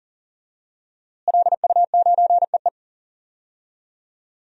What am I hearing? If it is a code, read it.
Morse code, heard as a keyed tone, about 40 wpm, 700 Hz. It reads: LU9EE
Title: DXCC-100 Call Signs - 5 Characters - Repeated 2x - 40wpm